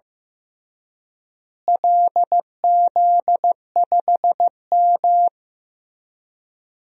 Morse code, heard as a keyed tone, about 15 wpm, 700 Hz. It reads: LZ5M
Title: DXCC-100 Call Signs - 4 Characters - Repeated 3x - 15wpm